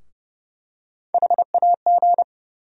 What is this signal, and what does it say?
Morse code, heard as a keyed tone, about 30 wpm, 700 Hz. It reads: HAG
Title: USA Call Sign Postfixes - 3 Letters 30wpm